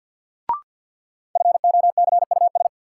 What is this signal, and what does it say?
Morse code, heard as a keyed tone, about 51 wpm, 700 Hz. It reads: FOCUS
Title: Sentences from Top 1000 Words - Review Part 4 - New Word Drill - 50wpm